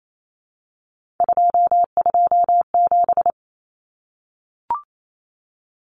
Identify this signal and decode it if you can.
Morse code, heard as a keyed tone, about 28 wpm, 700 Hz. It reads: 227
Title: Common Cut Numbers - 3-Digit 28wpm